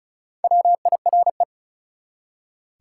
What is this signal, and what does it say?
Morse code, heard as a keyed tone, about 33 wpm, 700 Hz. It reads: WIRE
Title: Top 180 QSO Elements 35wpm